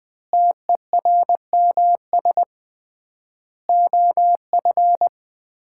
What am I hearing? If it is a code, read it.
Morse code, heard as a keyed tone, about 20 wpm, 700 Hz. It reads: TERMS OF
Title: 3x Word-Spacing - Sets of 4 Words - 20wpm